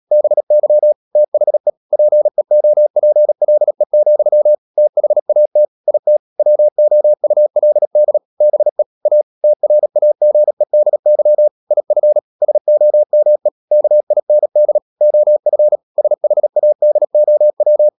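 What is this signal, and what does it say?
Morse code, heard as a keyed tone, about 37 wpm, 600 Hz. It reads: BY THE PEOPLE, THAT IT WOULD BE A TRAGEDY IF SOME KIND OF SHADOW
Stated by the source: Morse code practice at 35–39 wpm sending English text of Ken Follett's novel Never, from CW med Gnister.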